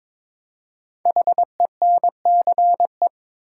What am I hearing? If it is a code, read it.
Morse code, heard as a keyed tone, about 22 wpm, 690 Hz. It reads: HENCE
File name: 5 Letter Words - First 3 Letters Given - 22wpm